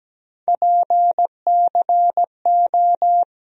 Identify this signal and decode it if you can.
Morse code, heard as a keyed tone, about 17 wpm, 700 Hz. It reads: PCO
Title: USA Call Sign Postfixes - 3 Letters 17wpm